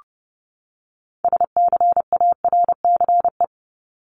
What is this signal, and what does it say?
Morse code, heard as a keyed tone, about 30 wpm, 700 Hz. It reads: SCARCE
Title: LY Suffix - Top 100 Words with Base Words 30wpm